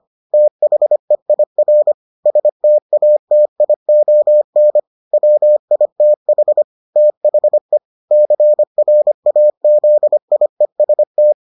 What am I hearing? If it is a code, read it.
Morse code, heard as a keyed tone, about 25 wpm, 600 Hz. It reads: THEIR STATION WITH THE CRAZIEST